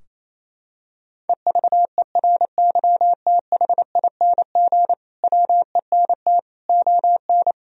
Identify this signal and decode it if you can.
Morse code, heard as a keyed tone, about 27 wpm, 700 Hz. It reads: EVERYTHING WENT ON
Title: Sentences from Top 1000 Words - Review Part 4 - 28wpm